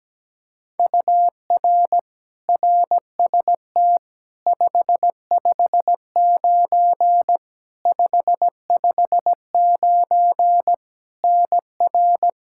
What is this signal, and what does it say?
Morse code, heard as a keyed tone, about 17 wpm, 700 Hz. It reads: UR RST 559 559 NR
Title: Practice QSO - Exchange 1 - Beginning - 17wpm